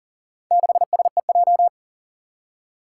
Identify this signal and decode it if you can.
Morse code, heard as a keyed tone, about 40 wpm, 700 Hz. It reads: 6SEJ
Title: Four Challenging Letter-Number Combinations 40wpm